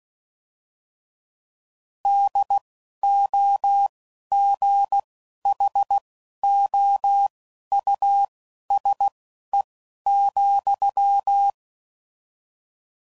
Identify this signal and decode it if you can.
Morse code, heard as a keyed tone, about 16 wpm, spaced as for 11 wpm, 800 Hz. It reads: DOGHOUSE,